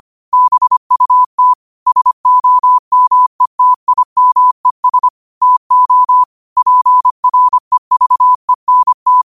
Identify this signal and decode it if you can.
Morse code, heard as a keyed tone, about 25 wpm, 1000 Hz. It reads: BUT SOMETIMES TO PREVENT
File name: qotdfiles / 25wpm